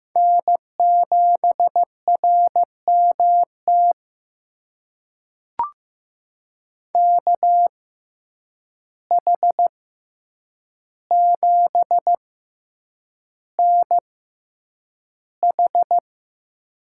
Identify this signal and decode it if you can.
Morse code, heard as a keyed tone, about 15 wpm, spaced as for 8 wpm, 700 Hz. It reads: N7RMT KH7NH
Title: USA Call Signs - 6x Character Spacing - 15wpm